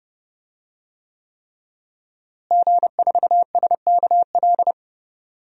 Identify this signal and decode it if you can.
Morse code, heard as a keyed tone, about 32 wpm, 700 Hz. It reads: G4SKL
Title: DXCC-100 Call Signs - 5 Characters - Repeated 3x - 30wpm